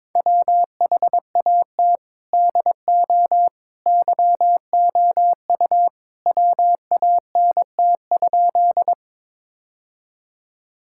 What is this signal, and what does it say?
Morse code, heard as a keyed tone, about 22 wpm, 700 Hz. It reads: WHAT DO YOU WANT?
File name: practice / 22wpm / Sentences with 2-Word Phrases - 22wpm